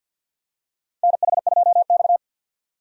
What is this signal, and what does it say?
Morse code, heard as a keyed tone, about 48 wpm, 700 Hz. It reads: NH2X
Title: USA Call Signs - 4 Characters - 50wpm